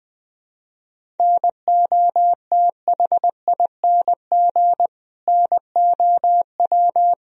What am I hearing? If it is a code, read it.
Morse code, heard as a keyed tone, about 20 wpm, 700 Hz. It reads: NOTHING NOW
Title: Sentences from Top 1000 Words - Review Part 1 - 20wpm